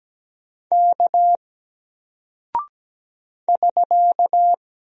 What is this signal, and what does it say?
Morse code, heard as a keyed tone, about 17 wpm, 700 Hz. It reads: K <SK>
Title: Single Character (Letter:Number:Punctuation:Prosign) Alt 17wpm